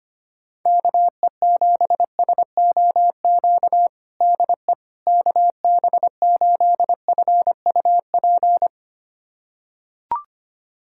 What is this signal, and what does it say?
Morse code, heard as a keyed tone, about 25 wpm, 700 Hz. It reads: KE7SOQ DE KB8FUP
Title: USA Callsign DE Callsign 25wpm